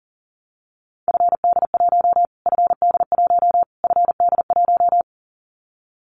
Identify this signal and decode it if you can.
Morse code, heard as a keyed tone, about 40 wpm, 700 Hz. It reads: FD1 FD1 FD1
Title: Callsign Prefixes for Top 25 DXCC - Repeated 3x - 40wpm